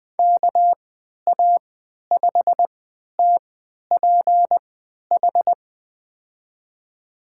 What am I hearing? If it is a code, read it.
Morse code, heard as a keyed tone, about 20 wpm, spaced as for 11 wpm, 700 Hz. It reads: KA5TPH
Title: USA Call Signs - 6 Characters - 3x Character Spacing - 20wpm